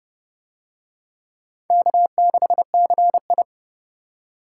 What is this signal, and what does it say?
Morse code, heard as a keyed tone, about 29 wpm, 700 Hz. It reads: K6CI